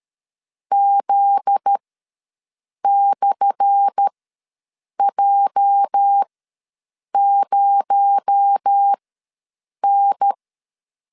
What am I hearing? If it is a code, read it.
Morse code, heard as a keyed tone, about 13 wpm, spaced as for 7 wpm, 800 Hz. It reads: Z/J0N
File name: LEZIONI / 10 SIGNS / 10A SLASH ws 12wpm 3443